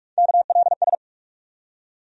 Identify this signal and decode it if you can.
Morse code, heard as a keyed tone, about 42 wpm, 700 Hz. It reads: KLS
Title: USA Call Sign Postfixes - 3 Letters 45wpm